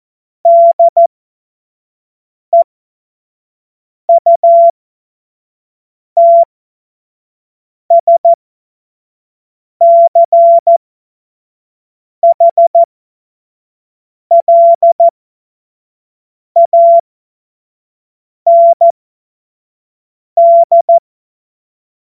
Morse code, heard as a keyed tone, about 14 wpm, spaced as for 5 wpm, 680 Hz. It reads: DEUTSCHLAND
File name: Morsekurs / Witze / Witz 09. Dezember 2025 25 F70